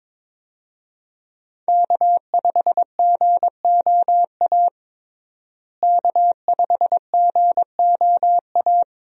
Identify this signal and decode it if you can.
Morse code, heard as a keyed tone, about 22 wpm, 700 Hz. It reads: K5GOA K5GOA
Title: USA Call Signs - 5 Characters - Repeated 3x - 22wpm